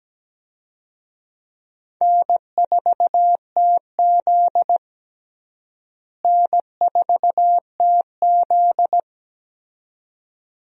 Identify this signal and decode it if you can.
Morse code, heard as a keyed tone, about 17 wpm, 700 Hz. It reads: N4TZ N4TZ
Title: USA Call Signs - 4 Characters - Repeated 2x - 17wpm